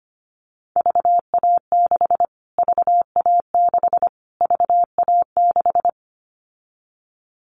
Morse code, heard as a keyed tone, about 25 wpm, 700 Hz. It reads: VA6 VA6 VA6
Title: Callsign Prefixes for Top 25 DXCC - Repeated 3x - 25wpm